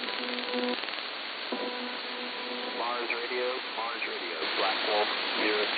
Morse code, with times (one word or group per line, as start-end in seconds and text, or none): none